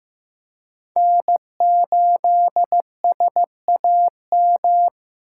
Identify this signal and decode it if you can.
Morse code, heard as a keyed tone, about 15 wpm, 700 Hz. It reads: N8SAM
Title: USA Call Signs - 5 Characters - 4x Character Spacing - 15wpm